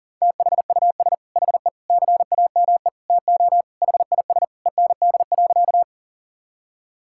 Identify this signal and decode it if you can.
Morse code, heard as a keyed tone, about 40 wpm, 700 Hz. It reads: THUS HE CAME TO HIS END.
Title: Sentences from Top 1000 Words - Review Part 5 - 40wpm